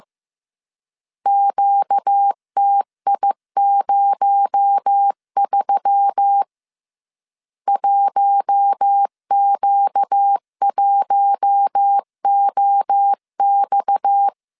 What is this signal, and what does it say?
Morse code, heard as a keyed tone, about 15 wpm, 800 Hz. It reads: QTI03 1Q1OX
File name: CW / DET / 1 6 1500C S C 14wpm 2951